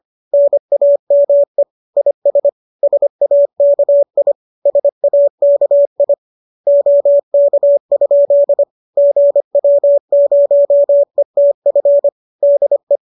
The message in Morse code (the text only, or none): NAME IS SAKI SAKI OK? GW0ETF DE